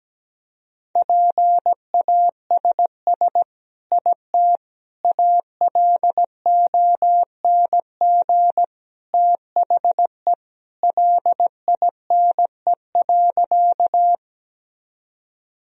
Morse code, heard as a keyed tone, about 17 wpm, 700 Hz. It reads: PASS IT ALONG THE LINE.
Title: Sentences from Top 1000 Words - Review Part 1 - 17wpm